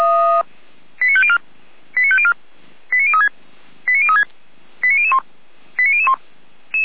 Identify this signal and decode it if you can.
Five-tone selective call (ZVEI), heard as quick runs of five tones, tones of about 70 ms. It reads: 89504 89504 89036 89036 89001 89001
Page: feuerwehr-orlamuende